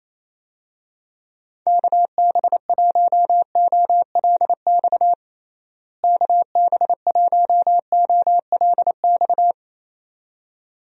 Morse code, heard as a keyed tone, about 27 wpm, 700 Hz. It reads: KB1OLX KB1OLX
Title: USA Call Signs - 6 Characters - Repeated 2x - 28wpm